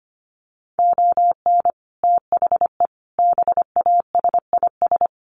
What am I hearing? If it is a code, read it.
Morse code, heard as a keyed tone, about 25 wpm, 700 Hz. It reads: ON THE BASIS
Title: Sets of 5 Words Encore 25wpm